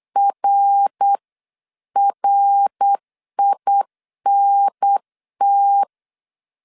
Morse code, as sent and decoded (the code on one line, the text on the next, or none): .-. .-. .. -. -
RRINT